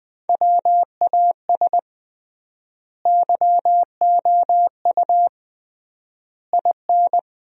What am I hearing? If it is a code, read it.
Morse code, heard as a keyed tone, about 20 wpm, 700 Hz. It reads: WAS YOU IN
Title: 3x Word-Spacing - Sentences from Top 300 Words - 20wpm